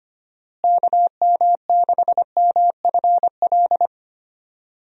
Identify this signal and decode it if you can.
Morse code, heard as a keyed tone, about 25 wpm, 700 Hz. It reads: KM6MFL
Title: USA Call Signs - 6 Characters - Repeated 2x - 25wpm